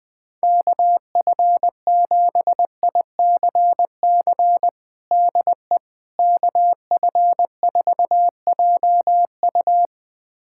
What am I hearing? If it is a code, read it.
Morse code, heard as a keyed tone, about 20 wpm, 700 Hz. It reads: KF7ICC DE KF4JU
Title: USA Callsign DE Callsign 20wpm